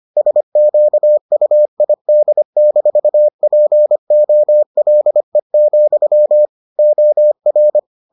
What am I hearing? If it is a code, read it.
Morse code, heard as a keyed tone, about 25 wpm, 600 Hz. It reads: SQUID-POLE, OR